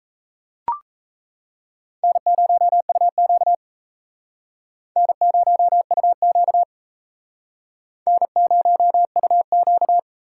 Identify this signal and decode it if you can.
Morse code, heard as a keyed tone, about 36 wpm, 700 Hz. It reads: N0UQ N0UQ N0UQ